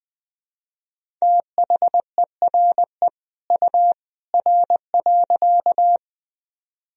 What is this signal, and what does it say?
Morse code, heard as a keyed tone, about 20 wpm, 700 Hz. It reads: THERE U R.
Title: Sentences from Top 100 Words and Abbreviations 20wpm